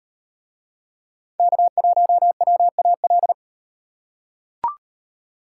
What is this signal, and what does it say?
Morse code, heard as a keyed tone, about 38 wpm, 700 Hz. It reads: K1WAL